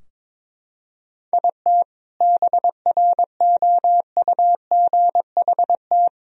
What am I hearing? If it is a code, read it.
Morse code, heard as a keyed tone, about 21 wpm, 700 Hz. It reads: IT BROUGHT
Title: Sentences from Top 1000 Words - Review Part 3 - 22wpm